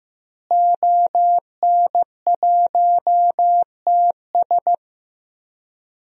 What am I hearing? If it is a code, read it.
Morse code, heard as a keyed tone, about 15 wpm, 700 Hz. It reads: ON1TS
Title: DXCC-100 Call Signs - 5 Characters - Repeated 2x - 15wpm